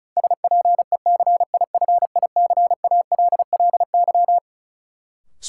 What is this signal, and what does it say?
Morse code, heard as a keyed tone, about 36 wpm, 700 Hz. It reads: SPECIFICALLY
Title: Sentences from Top 1100 Words - New Word Drill Alt - 35wpm